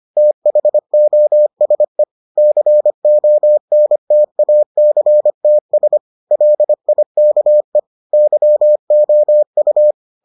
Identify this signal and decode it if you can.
Morse code, heard as a keyed tone, about 25 wpm, 600 Hz. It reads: THOSE CONTACTS LIKE YOU